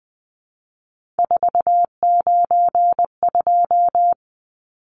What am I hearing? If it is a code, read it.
Morse code, heard as a keyed tone, about 20 wpm, 700 Hz. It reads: 492
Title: Numbers - 3-Digit 20wpm